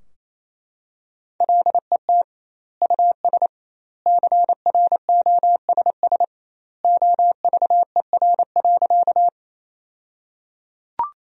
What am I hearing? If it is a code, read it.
Morse code, heard as a keyed tone, about 28 wpm, 700 Hz. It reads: LET US CROSS OVER.
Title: 2x Word-Spacing - Sentences from Top 300 Words - 28wpm